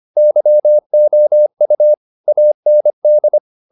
Morse code, heard as a keyed tone, about 25 wpm, 600 Hz. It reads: YOU AND
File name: the old man and the sea part 1 25wpm